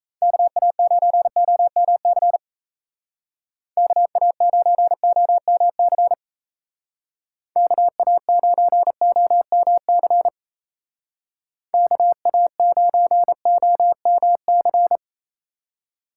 Morse code, heard as a keyed tone, about 34 wpm, 700 Hz. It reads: KA9OMC KA9OMC KA9OMC KA9OMC
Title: Speed Racing - USA Call Signs 28wpm